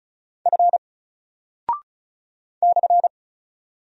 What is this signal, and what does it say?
Morse code, heard as a keyed tone, about 36 wpm, 700 Hz. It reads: F /
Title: Single Character (Letter:Number:Punctuation:Prosign) 35wpm